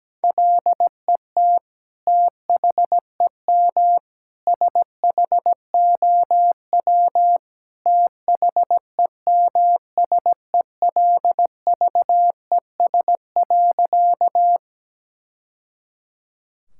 Morse code, heard as a keyed tone, about 17 wpm, 700 Hz. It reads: LET THEM SHOW THEMSELVES.
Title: Sentences from Top 1000 Words - Review Part 5 - 17wpm